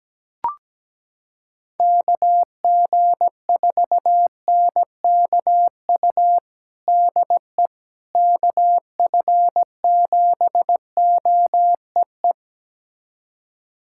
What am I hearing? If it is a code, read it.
Morse code, heard as a keyed tone, about 17 wpm, 700 Hz. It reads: KG4NKU DE KF7OEE
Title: USA Callsign DE Callsign - 6 Characters - 17wpm